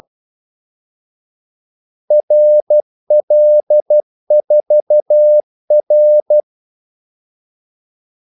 Morse code, heard as a keyed tone, about 12 wpm, 600 Hz. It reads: RL4R
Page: Kaldesignaler 12 wpm